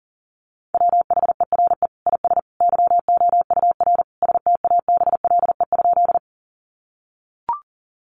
Morse code, heard as a keyed tone, about 40 wpm, 700 Hz. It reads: WHERE IS YOUR STABLE?
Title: Sentences from Top 1500 Words 40wpm